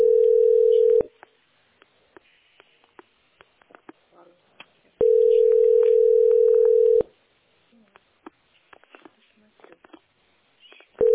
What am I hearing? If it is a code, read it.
Ringback tone (North America), 440 + 480 Hz, with speech.
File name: d9f76300-981b-46b7-9efb